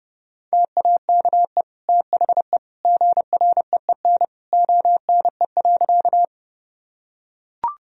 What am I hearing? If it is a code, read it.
Morse code, heard as a keyed tone, about 32 wpm, 700 Hz. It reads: TAKE THE GREEN ONE.
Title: Sentences from Top 1000 Words - Review Part 2 - 30wpm